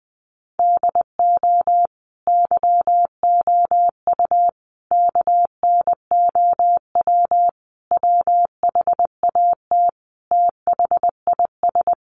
Morse code, heard as a keyed tone, about 20 wpm, 700 Hz. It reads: DO YOU KNOW WHAT THIS
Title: Sentences from Top 1500 Words 20wpm